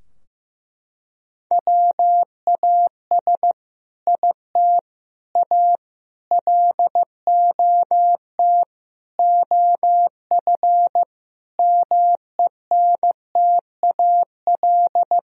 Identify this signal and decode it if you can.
Morse code, heard as a keyed tone, about 15 wpm, 700 Hz. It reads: WAS IT A LOT OF MENTAL
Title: Sentences from Top 1000 Words - Review Part 4 - 15wpm